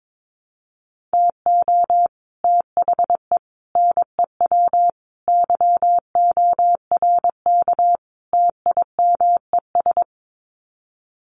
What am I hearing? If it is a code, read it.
Morse code, heard as a keyed tone, about 22 wpm, 700 Hz. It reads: TO THE NEW YORK TIMES
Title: Sets of 5 Words Encore2 22wpm